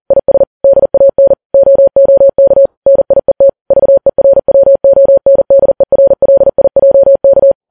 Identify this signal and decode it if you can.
Morse code, heard as a keyed tone, about 40 wpm, 550 Hz. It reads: IS DAN OOK NIET VERWONDERLIJK